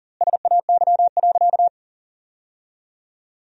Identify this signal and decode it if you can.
Morse code, heard as a keyed tone, about 40 wpm, 700 Hz. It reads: SAY.
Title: 2x Word-Spacing - Sentences from Top 500 Words - 40wpm